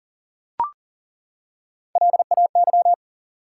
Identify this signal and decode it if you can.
Morse code, heard as a keyed tone, about 40 wpm, 700 Hz. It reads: LAY
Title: Three Letter Words - 40wpm